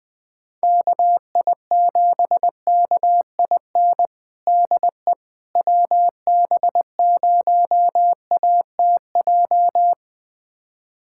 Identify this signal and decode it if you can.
Morse code, heard as a keyed tone, about 20 wpm, 700 Hz. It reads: KI7KIN DE WB0ATJ
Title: USA Callsign DE Callsign 20wpm